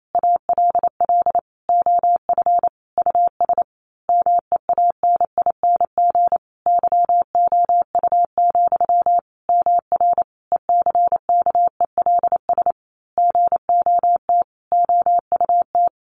Morse code, heard as a keyed tone, about 28 wpm, 700 Hz. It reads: ALL OF US MEANING YOU, MR ECKELS GOT OUT
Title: A Sound Of Thunder 28wpm